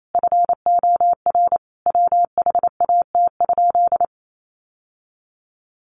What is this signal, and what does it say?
Morse code, heard as a keyed tone, about 28 wpm, 700 Hz. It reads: FOR WHAT?
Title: Sentences from Top 700 Words 28wpm